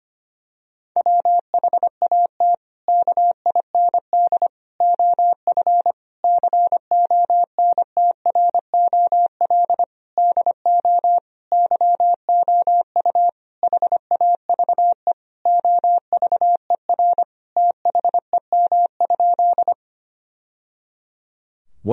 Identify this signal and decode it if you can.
Morse code, heard as a keyed tone, about 25 wpm, 700 Hz. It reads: WHAT KIND OF CONTROL DO YOU HAVE OVER THEM?
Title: Sentences from Top 1000 Words - Review Part 3 - 25wpm